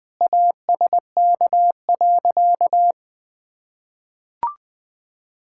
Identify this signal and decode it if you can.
Morse code, heard as a keyed tone, about 20 wpm, 700 Hz. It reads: ASK.
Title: Sentences with 3-Word Phrases - Spoken Beginning - 20wpm